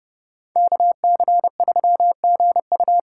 Morse code, heard as a keyed tone, about 36 wpm, 700 Hz. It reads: KC3GU
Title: Speed Racing - USA 2x2 Call Signs 30wpm